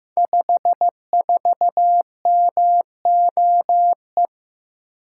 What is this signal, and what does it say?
Morse code, heard as a keyed tone, about 15 wpm, 700 Hz. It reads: 54MOE
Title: Code Groups - 5 Letter:Numbers - 15wpm